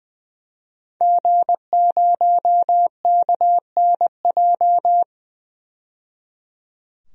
Morse code, heard as a keyed tone, about 20 wpm, 700 Hz. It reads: G0KNJ
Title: DXCC-100 Call Signs - 5 Characters - 20wpm